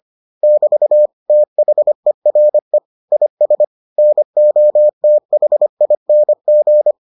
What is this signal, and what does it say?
Morse code, heard as a keyed tone, about 25 wpm, 600 Hz. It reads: <BT> THERE IS NOTHING